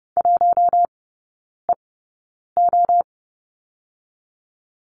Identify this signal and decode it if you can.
Morse code, heard as a keyed tone, about 30 wpm, spaced as for 9 wpm, 700 Hz. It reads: 1EO
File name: Lesson 18e - Letter m - Three Characters 30wpm